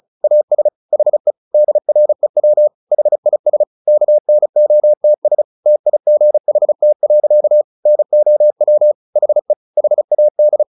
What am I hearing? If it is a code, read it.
Morse code, heard as a keyed tone, about 34 wpm, 600 Hz. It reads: AS HE DREW HIS KNOTS TIGHT. NOW HE HAD
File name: the old man and the sea / the old man and the sea 1 35wpm